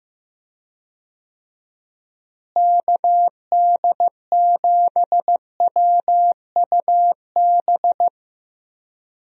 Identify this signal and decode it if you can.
Morse code, heard as a keyed tone, about 15 wpm, 700 Hz. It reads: KD7WUB
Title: USA Call Signs - 6 Characters - Repeated 3x - 15wpm